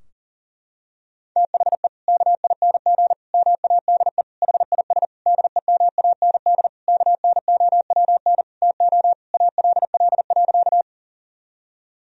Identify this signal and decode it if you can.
Morse code, heard as a keyed tone, about 40 wpm, 700 Hz. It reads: THE KING MADE HIS DEMAND KNOWN TO ALL.
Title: Sentences from Top 1000 Words - Review Part 4 - 40wpm